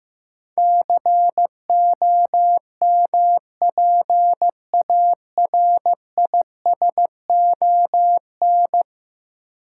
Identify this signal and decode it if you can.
Morse code, heard as a keyed tone, about 15 wpm, 700 Hz. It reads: COMPARISON
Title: Sentences from Top 1600 Words - New Word Drill - 15wpm